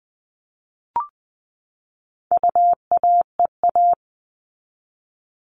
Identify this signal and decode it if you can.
Morse code, heard as a keyed tone, about 20 wpm, 700 Hz. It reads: UAEA
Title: All Cut Numbers - 4-Digit 20wpm